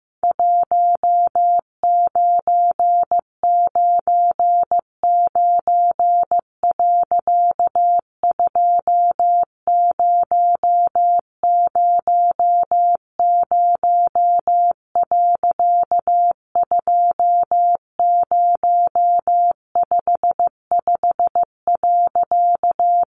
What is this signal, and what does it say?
Morse code, heard as a keyed tone, about 14 wpm, 700 Hz. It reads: 1999. 2000. 2055.
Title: A Sound Of Thunder 15wpm